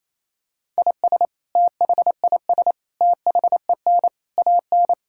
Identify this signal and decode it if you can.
Morse code, heard as a keyed tone, about 28 wpm, 700 Hz. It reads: IS THIS THEN AN
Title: Sentences from Top 1000 Words - Review Part 5 - 28wpm